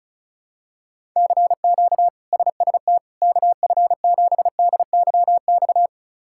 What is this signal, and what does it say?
Morse code, heard as a keyed tone, about 36 wpm, 700 Hz. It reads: CQ SST KF7DYX
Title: K1USN SST QSO Practice - 35wpm